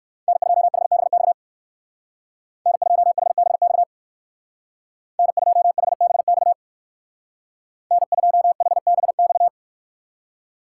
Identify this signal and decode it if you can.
Morse code, heard as a keyed tone, about 58 wpm, 700 Hz. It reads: N2HBX N2HBX N2HBX N2HBX